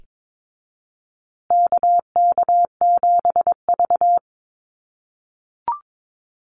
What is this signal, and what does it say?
Morse code, heard as a keyed tone, about 22 wpm, 700 Hz. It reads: KK7V
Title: USA Call Signs Repeated 2x 22wpm